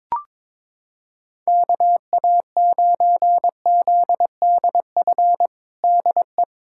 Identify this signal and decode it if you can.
Morse code, heard as a keyed tone, about 22 wpm, 700 Hz. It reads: KA9ZDF DE
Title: USA Callsign DE Callsign - 6 Characters - 22wpm